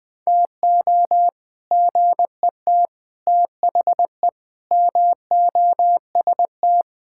Morse code, heard as a keyed tone, about 21 wpm, 700 Hz. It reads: TO GET THE MOST
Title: Sets of 6 Words - 20wpm